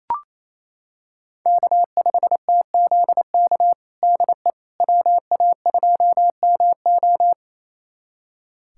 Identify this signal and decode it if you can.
Morse code, heard as a keyed tone, about 28 wpm, 700 Hz. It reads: K5TZK DE WA2MO
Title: USA Callsign DE Callsign - 5 Characters - 28wpm